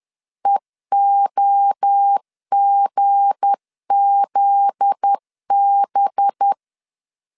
Morse code, heard as a keyed tone, about 10 wpm, 800 Hz. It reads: EOGZB
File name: LEZIONI / 8 YJ45 / 8 YJ 10wpm 2651